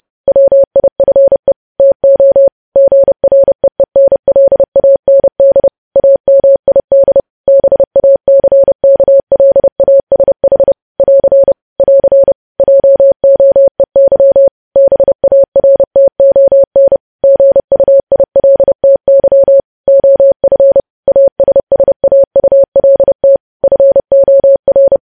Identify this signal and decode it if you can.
Morse code, heard as a keyed tone, about 30 wpm, 550 Hz. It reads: WIFE TO GREENLAND AMID BACKLASH <AR> <AR> JOEY BARTON GUILTY OF ASSAULT FOR
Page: news headlines in morse code at 30 wpm